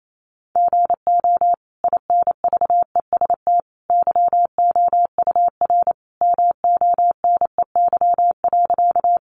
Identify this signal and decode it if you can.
Morse code, heard as a keyed tone, about 28 wpm, 700 Hz. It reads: GO INVEST YOUR MONEY.
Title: Sentences from Top 1100 Words 28wpm